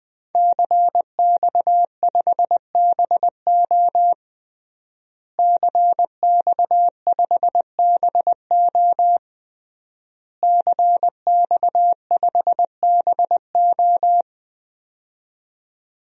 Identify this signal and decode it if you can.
Morse code, heard as a keyed tone, about 21 wpm, 700 Hz. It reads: CX5BO CX5BO CX5BO
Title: DXCC-100 Call Signs - 5 Characters - Repeated 3x - 20wpm